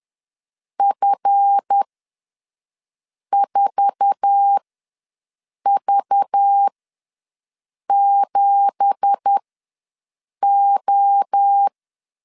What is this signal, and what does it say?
Morse code, heard as a keyed tone, about 10 wpm, spaced as for 5 wpm, 800 Hz. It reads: F4V7O
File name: LEZIONI / 10 SIGNS / 10B INT ws 10wpm 4134